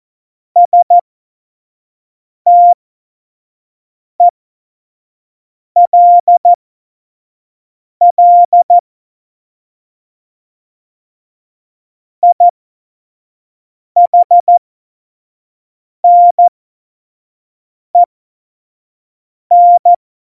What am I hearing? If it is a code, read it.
Morse code, heard as a keyed tone, about 14 wpm, spaced as for 5 wpm, 700 Hz. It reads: STELL IHNEN
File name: Morsekurs / Witze / Witz 08. September 2020, 25 F70